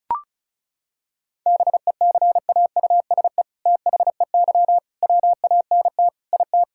Morse code, heard as a keyed tone, about 34 wpm, 700 Hz. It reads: BECAUSE THEY WANT IT
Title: Sentences from Top 1000 Words - Review Part 5 - 35wpm